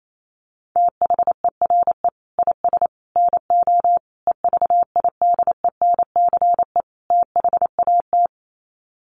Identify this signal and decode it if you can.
Morse code, heard as a keyed tone, about 28 wpm, 700 Hz. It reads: THERE IS NO EVIDENCE THAT
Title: Sets of 5 Words Encore 28wpm